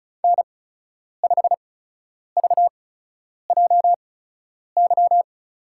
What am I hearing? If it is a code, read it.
Morse code, heard as a keyed tone, about 34 wpm, spaced as for 10 wpm, 700 Hz. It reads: N5VJY
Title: USA Call Signs - 5 Characters - 8x Character Spacing - 35wpm